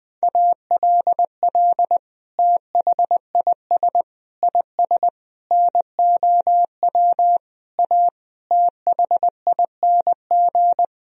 Morse code, heard as a keyed tone, about 20 wpm, 700 Hz. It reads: ALL THIS IS NOW A THING
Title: Sentences from Top 1000 Words - Review Part 3 - 20wpm